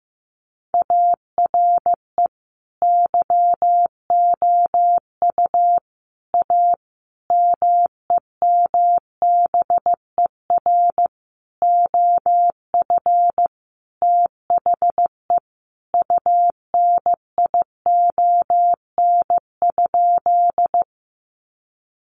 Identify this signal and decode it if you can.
Morse code, heard as a keyed tone, about 15 wpm, 700 Hz. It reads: ARE YOU A MEMBER OF THE UNION?
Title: Sentences from Top 1400 Words 15wpm